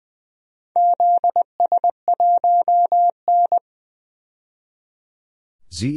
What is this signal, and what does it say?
Morse code, heard as a keyed tone, about 20 wpm, 700 Hz. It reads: ZS1N
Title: DXCC-100 Call Signs - 4 Characters - 20wpm